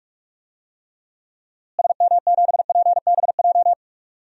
Morse code, heard as a keyed tone, about 46 wpm, 700 Hz. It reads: SM7PBJ
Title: DXCC-100 Call Signs - 6 Characters - Repeated 2x - 45wpm